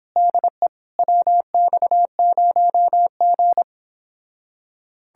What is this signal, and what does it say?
Morse code, heard as a keyed tone, about 24 wpm, 700 Hz. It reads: DE WX0G